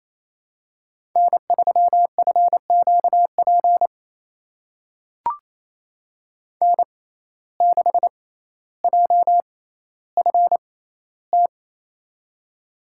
Morse code, heard as a keyed tone, about 28 wpm, spaced as for 14 wpm, 700 Hz. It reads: N3FQP N6JFT
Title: USA Call Signs - 5 Characters - 6x Character Spacing - 28wpm